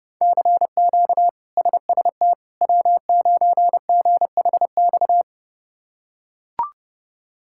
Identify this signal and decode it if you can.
Morse code, heard as a keyed tone, about 30 wpm, 700 Hz. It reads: CQ SST W9GHX
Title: K1USN SST QSO Practice - 30wpm